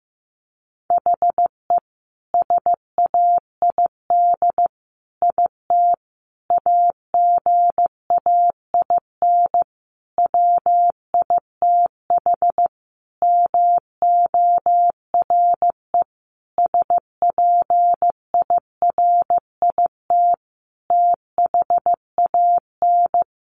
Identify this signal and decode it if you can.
Morse code, heard as a keyed tone, about 15 wpm, 700 Hz. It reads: HE SAID IT AGAIN WITH MORE SPIRIT THAN
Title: Sentences from Top 1400 Words 15wpm